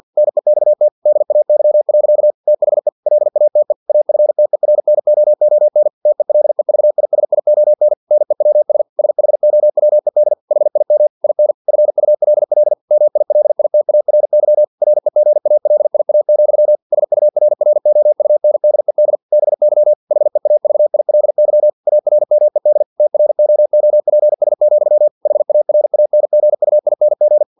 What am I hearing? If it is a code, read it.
Morse code, heard as a keyed tone, about 50 wpm, 600 Hz. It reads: NEXT DAY. THE LATE AFTERNOON TELEVISION NEWS SHOWED HIM IN FULL MILITARY REGALIA, SURROUNDED BY HEAVILY ARMED TROOPS, HARANGUING
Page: Never 49wpm